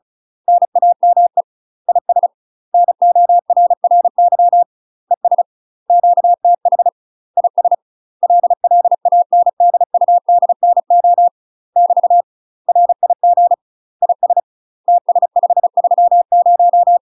Morse code, heard as a keyed tone, about 34 wpm, 700 Hz. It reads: NAME IS NORRY ES QTH IS LLANDUDNO <BT> RIG IS TS530
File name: UK MASTER QSOs TRAINING / 35WPM / UK MASTER QSOs 0004